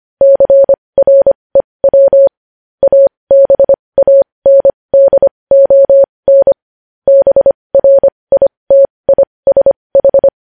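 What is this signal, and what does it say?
Morse code, heard as a keyed tone, about 25 wpm, spaced as for 20 wpm, 550 Hz. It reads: CREW ABANDON BRITISH
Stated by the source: Morse code news headlines at 20 wpm.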